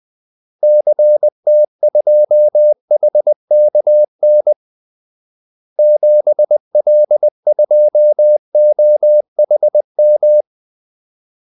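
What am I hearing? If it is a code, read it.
Morse code, heard as a keyed tone, about 20 wpm, 600 Hz. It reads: CT2HKN 7L2OHM
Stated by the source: Morse call-sign practice at 20 wpm.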